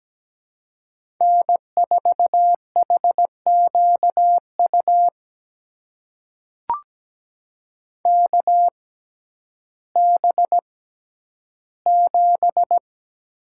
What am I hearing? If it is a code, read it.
Morse code, heard as a keyed tone, about 17 wpm, spaced as for 10 wpm, 700 Hz. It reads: N4HQU KB7
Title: USA Call Signs - 6x Character Spacing - 17wpm